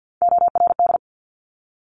Morse code, heard as a keyed tone, about 50 wpm, 700 Hz. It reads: ORD
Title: Top Three Letter Combinations in Words 50wpm